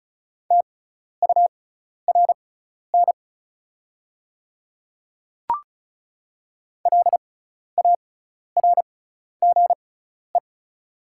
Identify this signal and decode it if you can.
Morse code, heard as a keyed tone, about 34 wpm, spaced as for 12 wpm, 700 Hz. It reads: TURN LARGE
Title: 6x Character Spacing - Top 500 Words - 35wpm